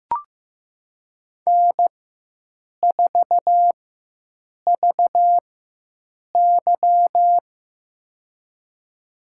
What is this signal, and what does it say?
Morse code, heard as a keyed tone, about 15 wpm, spaced as for 7 wpm, 700 Hz. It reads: N4VY
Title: USA Call Signs - 4x Character Spacing - 15wpm